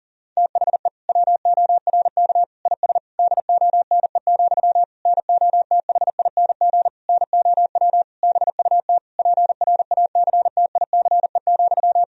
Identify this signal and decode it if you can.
Morse code, heard as a keyed tone, about 40 wpm, 700 Hz. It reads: THE WORK IS DONE, NOTHING NOW BUT PRACTICE,
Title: Sentences from Top 1000 Words - Review Part 4 - 40wpm